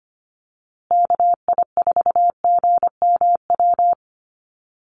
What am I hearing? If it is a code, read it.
Morse code, heard as a keyed tone, about 25 wpm, 700 Hz. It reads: KI4GMW
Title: USA Call Signs Repeated 3x 25wpm